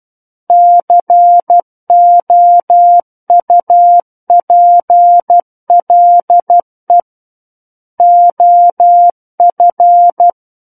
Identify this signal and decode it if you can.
Morse code, heard as a keyed tone, about 12 wpm, 700 Hz. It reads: COUPLE OF
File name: Book One / 03-Chapter 3 - 12 WPM